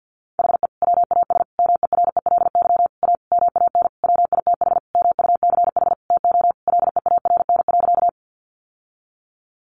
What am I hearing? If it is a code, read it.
Morse code, heard as a keyed tone, about 50 wpm, 700 Hz. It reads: HE WAS MERELY A MAN WITH MUCH TO LEARN.
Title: Sentences from Top 1400 Words 50wpm